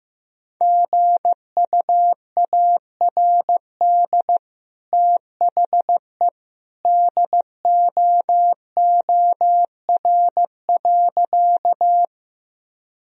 Morse code, heard as a keyed tone, about 15 wpm, 700 Hz. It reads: GUARD THE DOOR.
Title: Sentences from Top 1000 Words - Review Part 4 - 15wpm